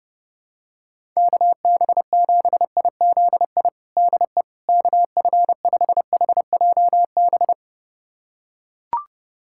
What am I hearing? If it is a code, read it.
Morse code, heard as a keyed tone, about 32 wpm, 700 Hz. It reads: KB7IZI DE KF5HJB
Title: USA Callsign DE Callsign - 6 Characters - 30wpm